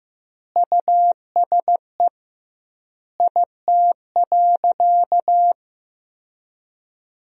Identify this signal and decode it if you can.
Morse code, heard as a keyed tone, about 15 wpm, 700 Hz. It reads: USE IT.
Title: 2x Word-Spacing - Sentences from Top 500 Words - 15wpm